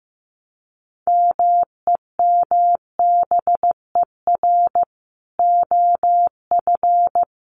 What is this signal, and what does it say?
Morse code, heard as a keyed tone, about 15 wpm, 700 Hz. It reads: MEMBER OF
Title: Sets of 3 Words 15wpm